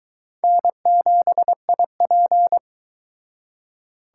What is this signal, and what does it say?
Morse code, heard as a keyed tone, about 23 wpm, 700 Hz. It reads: N7IP